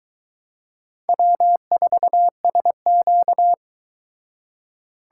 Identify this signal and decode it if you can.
Morse code, heard as a keyed tone, about 22 wpm, 700 Hz. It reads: W4SQ